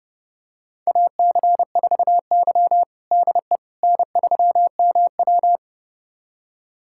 Morse code, heard as a keyed tone, about 30 wpm, 700 Hz. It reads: AC4Y DE N3MW
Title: USA Callsign DE Callsign - 4 Characters - 30wpm